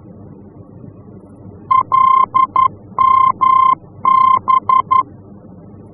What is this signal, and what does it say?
Morse code, heard as a keyed tone, about 11 wpm, 1050 Hz. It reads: LMB
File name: france / mx / LMB-F-VOR